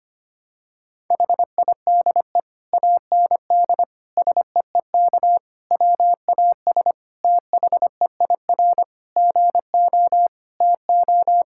Morse code, heard as a keyed tone, about 25 wpm, 700 Hz. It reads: HIDE AND SEEK WAS THEIR GO TO
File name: practice / 25wpm / Difficult Sentences with Hints - 25wpm